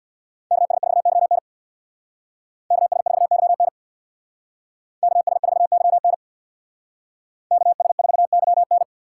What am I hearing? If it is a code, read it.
Morse code, heard as a keyed tone, about 64 wpm, 700 Hz. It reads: KS4YN KS4YN KS4YN KS4YN